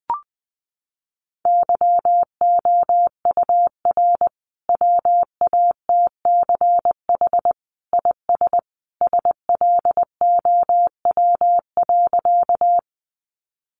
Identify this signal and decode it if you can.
Morse code, heard as a keyed tone, about 20 wpm, 700 Hz. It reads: YOUR WATCH IS SLOW.
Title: Sentences from Top 400 Words 20wpm